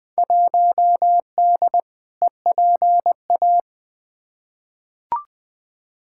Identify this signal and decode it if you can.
Morse code, heard as a keyed tone, about 20 wpm, 700 Hz. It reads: 1D EPA
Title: Field Day Exchanges - 20wpm